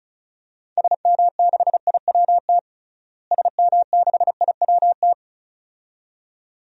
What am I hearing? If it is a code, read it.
Morse code, heard as a keyed tone, about 36 wpm, 700 Hz. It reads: SM6IWT SM6IWT
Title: DXCC-100 Call Signs - 6 Characters - Repeated 2x - 35wpm